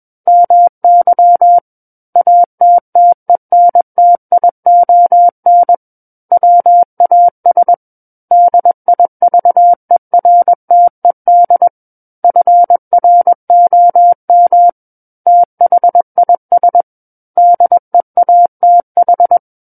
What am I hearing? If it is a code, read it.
Morse code, heard as a keyed tone, about 21 wpm, 700 Hz. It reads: MY ATTENTION WAS DIVERTED FROM THIS DEATH
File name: Book One / 12-Chapter 12 - 21 WPM